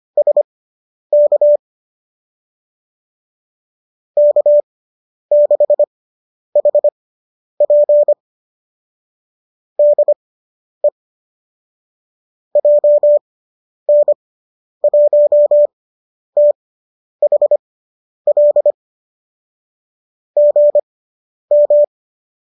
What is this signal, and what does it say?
Morse code, heard as a keyed tone, about 25 wpm, spaced as for 10 wpm, 570 Hz. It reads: SK K6HP DE JN1THL GM